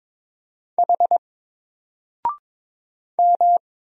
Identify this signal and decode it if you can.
Morse code, heard as a keyed tone, about 22 wpm, 700 Hz. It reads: H M
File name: Single-Letter-Number-22wpm